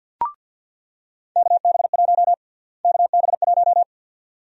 Morse code, heard as a keyed tone, about 48 wpm, 700 Hz. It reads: KB1 KB1
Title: USA Call Sign Prefixes - Repeated 2x - 50wpm